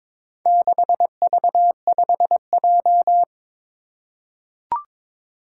Morse code, heard as a keyed tone, about 22 wpm, 700 Hz. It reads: 6V5J
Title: Four Challenging Letter-Number Combinations 22wpm